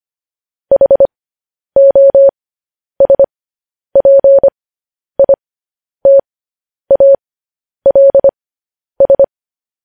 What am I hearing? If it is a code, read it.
Morse code, heard as a keyed tone, about 25 wpm, spaced as for 10 wpm, 550 Hz. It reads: HOSPITALS
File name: podcast / News headlines-1x-25x10